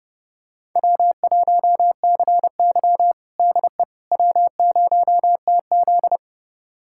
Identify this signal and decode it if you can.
Morse code, heard as a keyed tone, about 29 wpm, 700 Hz. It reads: W1CY DE W0TZ
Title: USA Callsign DE Callsign - 4 Characters - 30wpm